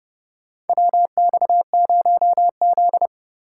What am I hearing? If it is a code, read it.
Morse code, heard as a keyed tone, about 32 wpm, 700 Hz. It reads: WX0Z
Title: K1USN SST QSO Practice - 30wpm